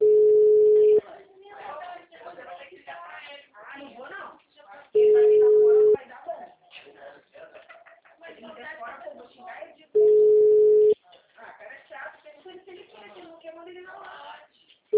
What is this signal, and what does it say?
Ringback tone (Europe, 425 Hz), with speech.